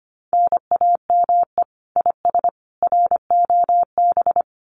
Morse code, heard as a keyed tone, about 25 wpm, 700 Hz. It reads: NAME IS ROB
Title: Common QSO Phrases 25wpm